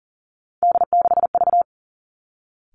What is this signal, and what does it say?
Morse code, heard as a keyed tone, about 40 wpm, 700 Hz. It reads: D6V
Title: Three Challenging Letter-Number Combinations 40wpm